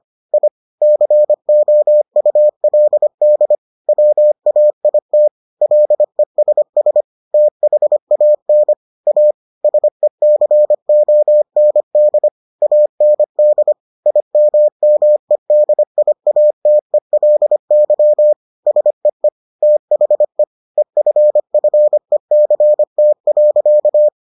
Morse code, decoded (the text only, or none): I COULD WAIT LESS THAN A SECOND AND IMMEDIATELY SEE THE EFFECT.